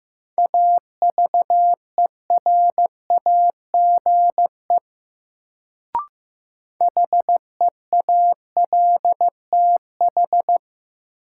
Morse code, heard as a keyed tone, about 15 wpm, 700 Hz. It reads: AVERAGE HEALTH
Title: Sets of 5 Words - New Word Drill - 15wpm